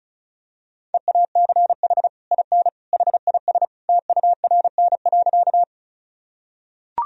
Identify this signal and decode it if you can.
Morse code, heard as a keyed tone, about 36 wpm, 700 Hz. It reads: EACH IN HIS TURN.
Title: Sentences from Top 1000 Words - Review Part 1 - 35wpm